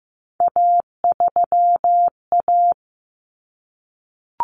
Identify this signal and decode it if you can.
Morse code, heard as a keyed tone, about 15 wpm, 700 Hz. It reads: A3A
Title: Common Cut Numbers - 3-Digit 15wpm